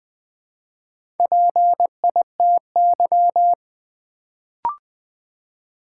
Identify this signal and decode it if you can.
Morse code, heard as a keyed tone, about 20 wpm, 700 Hz. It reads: PITY
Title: Build A Word - 1 Letter At A Time - 4-Letter Words - 20wpm